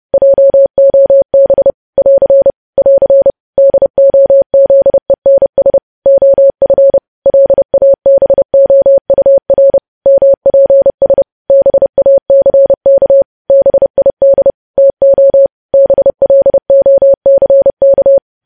Morse code, heard as a keyed tone, about 30 wpm, 550 Hz. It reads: JOB <AR> <AR> DOZENS OF LABOUR MPS BACK BID TO BLOCK
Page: news headlines in morse code at 30 wpm